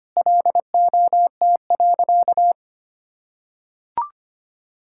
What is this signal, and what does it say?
Morse code, heard as a keyed tone, about 25 wpm, 700 Hz. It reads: LOT.
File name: practice / 25wpm / Sentences with 3-Word Phrases - 25wpm